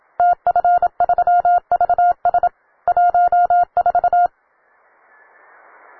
Morse code, heard as a keyed tone, about 27 wpm, 710 Hz. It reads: TF3VS 14
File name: wae / cw beginner / ua9cdv cw